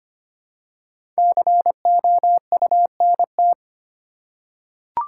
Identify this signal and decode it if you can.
Morse code, heard as a keyed tone, about 25 wpm, 700 Hz. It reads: COUNT
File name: 5 Letter Words - First 3 Letters Given - 25wpm